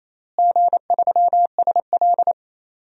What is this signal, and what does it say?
Morse code, heard as a keyed tone, about 28 wpm, 700 Hz. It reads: G3SL
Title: DXCC-100 Call Signs - 4 Characters - Repeated 2x - 28wpm